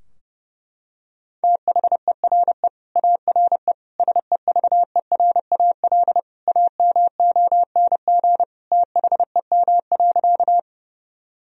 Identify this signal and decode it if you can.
Morse code, heard as a keyed tone, about 30 wpm, 700 Hz. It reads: THERE ARE SEVERAL AMONG THEM.
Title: Sentences from Top 1000 Words - Review Part 2 - 30wpm